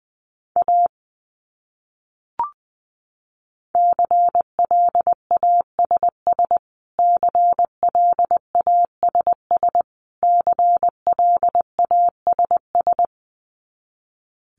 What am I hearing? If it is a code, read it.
Morse code, heard as a keyed tone, about 20 wpm, 700 Hz. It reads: A CLASS CLASS CLASS
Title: Top 500 Words Repeated 20wpm